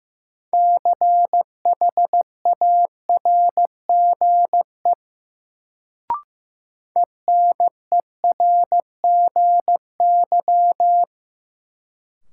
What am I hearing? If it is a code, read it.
Morse code, heard as a keyed tone, about 15 wpm, 700 Hz. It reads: CHARGE ENERGY
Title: Sentences from Top 900 Words - New Word Drill - 15wpm